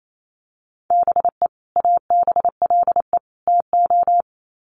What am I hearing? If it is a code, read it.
Morse code, heard as a keyed tone, about 28 wpm, 700 Hz. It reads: BE ABLE TO
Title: Sets of 3 Words 28wpm